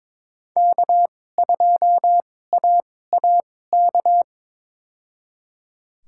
Morse code, heard as a keyed tone, about 22 wpm, spaced as for 16 wpm, 700 Hz. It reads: K2AAK
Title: USA Call Signs - 5 Characters - 2x Character Spacing - 22wpm